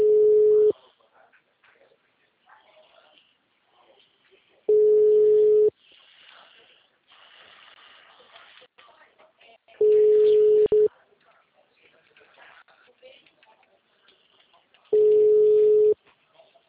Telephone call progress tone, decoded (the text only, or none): ringback tone